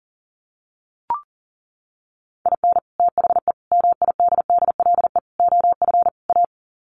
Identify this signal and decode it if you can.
Morse code, heard as a keyed tone, about 40 wpm, 700 Hz. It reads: IN THE MIDDLE OF A
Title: Sets of 5 Words 40wpm